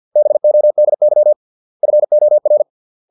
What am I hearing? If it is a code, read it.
Morse code, heard as a keyed tone, about 50 wpm, 600 Hz. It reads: BODY FOR